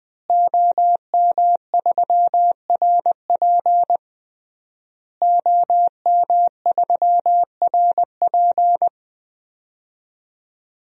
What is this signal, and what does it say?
Morse code, heard as a keyed tone, about 20 wpm, 700 Hz. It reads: OM3RP OM3RP
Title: DXCC-100 Call Signs - 5 Characters - Repeated 3x - 20wpm